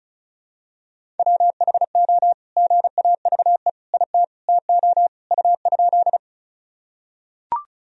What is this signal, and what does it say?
Morse code, heard as a keyed tone, about 36 wpm, 700 Hz. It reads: WHO GAVE IT TO U?
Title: Sentences from Top 500 Words and Abbreviations 35wpm